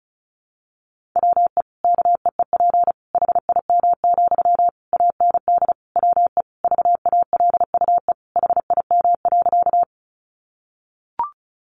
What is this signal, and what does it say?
Morse code, heard as a keyed tone, about 34 wpm, 700 Hz. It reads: WE KEEP HIM, AND WE VALUE HIM.
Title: Sentences from Top 600 Words 35wpm